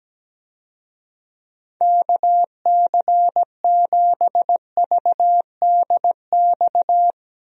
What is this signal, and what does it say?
Morse code, heard as a keyed tone, about 17 wpm, 700 Hz. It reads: KC7VDX
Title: USA Call Signs - 6 Characters - Repeated 3x - 17wpm